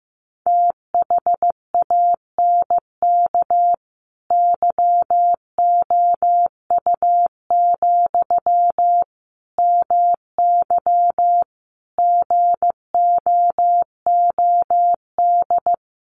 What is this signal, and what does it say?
Morse code, heard as a keyed tone, about 15 wpm, 700 Hz. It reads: THANK YOU, MY GOOD
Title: Sentences from Top 600 Words 15wpm